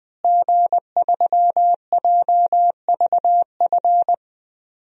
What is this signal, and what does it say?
Morse code, heard as a keyed tone, about 20 wpm, 700 Hz. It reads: G3JVF
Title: DXCC-100 Call Signs - 5 Characters - Repeated 2x - 20wpm